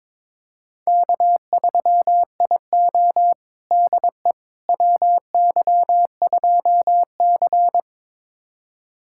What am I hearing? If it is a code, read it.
Morse code, heard as a keyed tone, about 22 wpm, 700 Hz. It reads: K3IO DE WY2C
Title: USA Callsign DE Callsign - 4 Characters - 22wpm